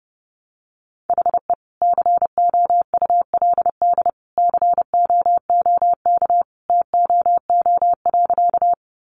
Morse code, heard as a keyed tone, about 30 wpm, 700 Hz. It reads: HE COULD COOK TOO.
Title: Sentences from Top 700 Words 30wpm